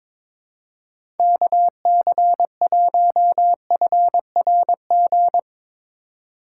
Morse code, heard as a keyed tone, about 21 wpm, 700 Hz. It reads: KC1FRG
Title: USA Call Signs - 6 Characters - 2x Character Spacing - 22wpm